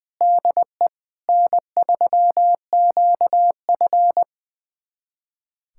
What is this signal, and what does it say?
Morse code, heard as a keyed tone, about 21 wpm, 700 Hz. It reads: DE N3QF
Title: USA Callsign DE Callsign - 4 Characters - 20wpm